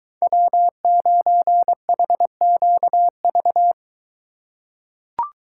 Morse code, heard as a keyed tone, about 22 wpm, 700 Hz. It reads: W9HQV